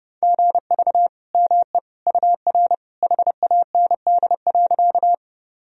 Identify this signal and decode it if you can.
Morse code, heard as a keyed tone, about 30 wpm, 700 Hz. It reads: GV ME UR HAND.
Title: Sentences from Top 200 Words and Abbreviations 30wpm